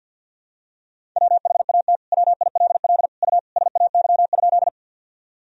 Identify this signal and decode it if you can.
Morse code, heard as a keyed tone, about 47 wpm, 700 Hz. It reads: WHAT WILL U SAY?
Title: Sentences from Top 200 Words and Abbreviations 50wpm